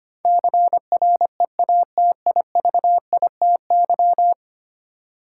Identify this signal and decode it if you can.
Morse code, heard as a keyed tone, about 25 wpm, 700 Hz. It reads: CREATIVITY
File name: Ten Letter Words - 25wpm